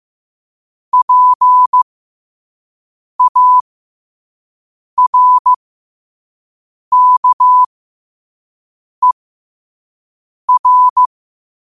Morse code, heard as a keyed tone, about 15 wpm, spaced as for 5 wpm, 1000 Hz. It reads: PARKER
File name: qotdfiles / 5wpm